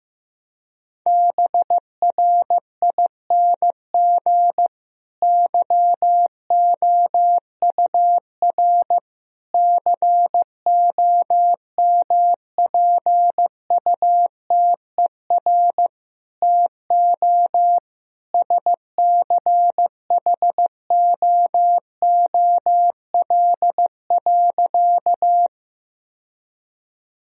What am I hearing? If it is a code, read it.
Morse code, heard as a keyed tone, about 15 wpm, 700 Hz. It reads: BRING YOUR COMPUTER TO SCHOOL.
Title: Sentences from Top 1000 Words - Review Part 3 - 15wpm